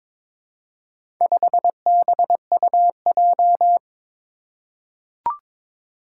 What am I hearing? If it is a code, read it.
Morse code, heard as a keyed tone, about 22 wpm, 700 Hz. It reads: 5BUJ
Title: Four Challenging Letter-Number Combinations 22wpm